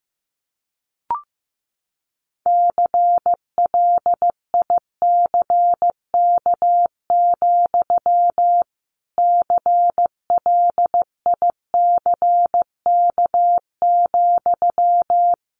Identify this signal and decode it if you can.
Morse code, heard as a keyed tone, about 14 wpm, 700 Hz. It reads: CLICK, CLICK,
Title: Sentences from Top 1100 Words 15wpm